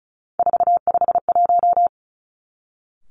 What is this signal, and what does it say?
Morse code, heard as a keyed tone, about 36 wpm, 700 Hz. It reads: 451
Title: Three Challenging Letter-Number Combinations 35wpm